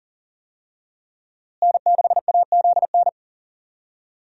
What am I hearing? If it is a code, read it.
Morse code, heard as a keyed tone, about 40 wpm, 700 Hz. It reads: N6AZN
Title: USA Call Signs - 5 Characters - Repeated 2x - 40wpm